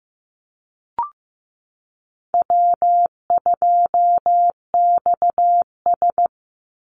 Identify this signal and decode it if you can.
Morse code, heard as a keyed tone, about 15 wpm, 700 Hz. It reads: W2XS
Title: Call Signs Repeated 3x - 15wpm